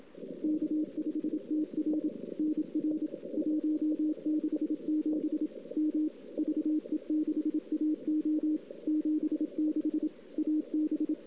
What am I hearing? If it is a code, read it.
Morse code, heard as a keyed tone, about 27 wpm, 310 Hz. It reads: K5TR NR 167 M VE6AO 76 AB
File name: CW / 2008-ss-cw-k5tr-op-n5rz -03:00